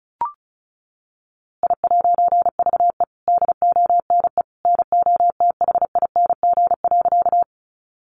Morse code, heard as a keyed tone, about 34 wpm, 700 Hz. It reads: I'VE DONE NOTHING.
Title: A Sound Of Thunder 35wpm